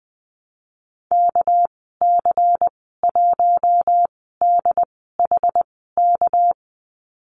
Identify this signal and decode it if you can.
Morse code, heard as a keyed tone, about 20 wpm, spaced as for 15 wpm, 700 Hz. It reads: KC1DHK
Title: 231 Lesson 37g - Letter x - Call Signs 20wpm